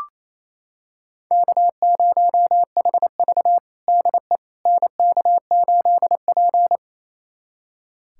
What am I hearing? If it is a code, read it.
Morse code, heard as a keyed tone, about 27 wpm, 700 Hz. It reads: K0HV DE NK8P
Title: USA Callsign DE Callsign - 4 Characters - 28wpm